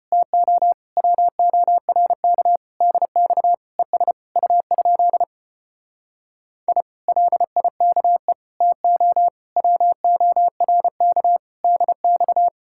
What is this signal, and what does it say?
Morse code, heard as a keyed tone, about 33 wpm, 700 Hz. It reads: TO WORK DX ES U? I LIKE TO WORK DX